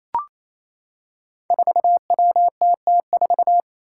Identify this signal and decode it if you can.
Morse code, heard as a keyed tone, about 28 wpm, 700 Hz. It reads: 4WTT4
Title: Five Challenging Letter-Number Combinations 28wpm